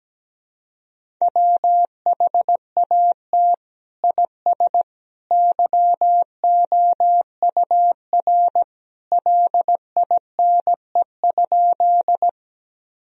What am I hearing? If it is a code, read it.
Morse code, heard as a keyed tone, about 17 wpm, 700 Hz. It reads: WHAT IS YOUR LINE?
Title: Sentences from Top 1000 Words - Review Part 1 - 17wpm